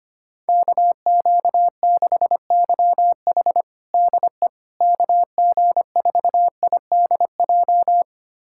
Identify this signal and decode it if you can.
Morse code, heard as a keyed tone, about 25 wpm, 700 Hz. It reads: KQ6YH DE KG4IDJ
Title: USA Callsign DE Callsign 25wpm